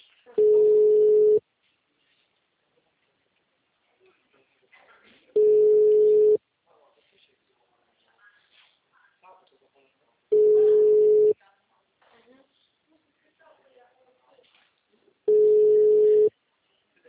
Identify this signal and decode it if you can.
Ringback tone (Europe, 425 Hz), with speech.
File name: out- -2002- - - .77171